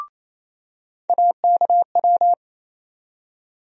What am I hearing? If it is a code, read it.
Morse code, heard as a keyed tone, about 28 wpm, 700 Hz. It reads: AKW
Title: USA Call Sign Postfixes - 3 Letters 28wpm